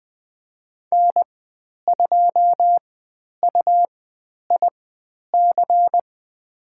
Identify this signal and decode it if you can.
Morse code, heard as a keyed tone, about 20 wpm, spaced as for 10 wpm, 700 Hz. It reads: N2UIC
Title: 102 Lesson 17g - Letter u - Call Signs 20wpm